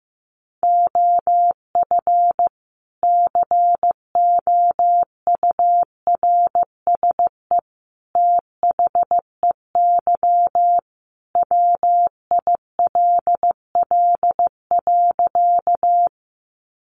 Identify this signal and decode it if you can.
Morse code, heard as a keyed tone, about 15 wpm, 700 Hz. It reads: OF COURSE THEY WILL.
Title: Sentences from Top 500 Words 15wpm